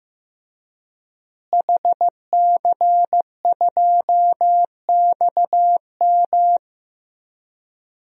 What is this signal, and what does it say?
Morse code, heard as a keyed tone, about 15 wpm, 700 Hz. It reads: HC2XM
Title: DXCC-100 Call Signs - 5 Characters - Repeated 2x - 15wpm